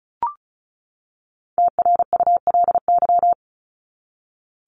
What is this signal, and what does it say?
Morse code, heard as a keyed tone, about 34 wpm, 700 Hz. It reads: TRULY
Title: LY Suffix - Top 100 Words with Base Words 35wpm